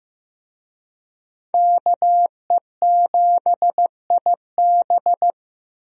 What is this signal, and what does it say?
Morse code, heard as a keyed tone, about 15 wpm, 700 Hz. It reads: KE7IB
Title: USA Call Signs - 5 Characters - Repeated 3x - 15wpm